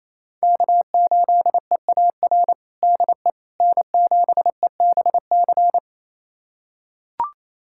Morse code, heard as a keyed tone, about 27 wpm, 700 Hz. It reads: K8EAR DE N7EBC
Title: USA Callsign DE Callsign - 5 Characters - 28wpm